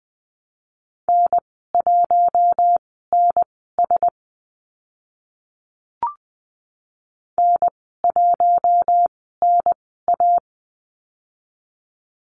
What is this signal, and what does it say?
Morse code, heard as a keyed tone, about 20 wpm, spaced as for 14 wpm, 700 Hz. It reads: N1NS N1NA
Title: 048 Lesson 9g - Number 4 - Call Signs 20wpm